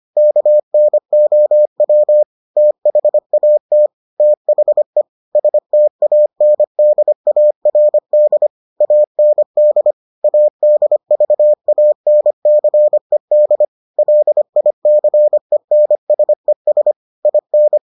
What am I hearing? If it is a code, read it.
Morse code, heard as a keyed tone, about 25 wpm, 600 Hz. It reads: KNOW THAT THE STANDARD AND ADVANCED LICENSES IN